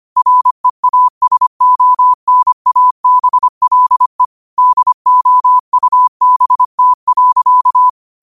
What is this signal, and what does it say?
Morse code, heard as a keyed tone, about 25 wpm, 1000 Hz. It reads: REASONABLE DOUBT.